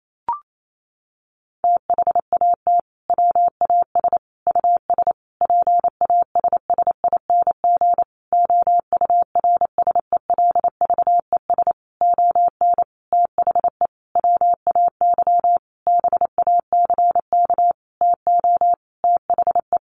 Morse code, heard as a keyed tone, about 28 wpm, 700 Hz. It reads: THAT WAS US PASSING OURSELVES ON THE WAY BACK TO THE
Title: A Sound Of Thunder 28wpm